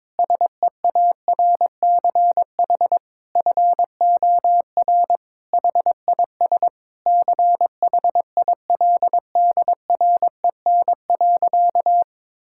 Morse code, heard as a keyed tone, about 22 wpm, 700 Hz. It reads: SEARCH FOR HIS CHILDREN.
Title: Sentences from Top 1000 Words - Review Part 5 - 22wpm